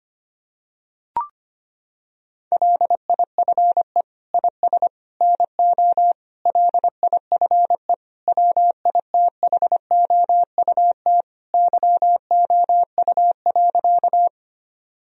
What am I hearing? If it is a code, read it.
Morse code, heard as a keyed tone, about 25 wpm, 700 Hz. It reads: LIFE IS NO LIFE WITHOUT YOU.
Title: Sentences from Top 1000 Words - Review Part 3 - 25wpm